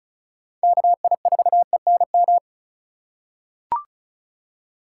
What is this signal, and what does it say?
Morse code, heard as a keyed tone, about 36 wpm, 700 Hz. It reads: KI4ENM
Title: USA Call Signs - 6 Characters - 8x Character Spacing - 35wpm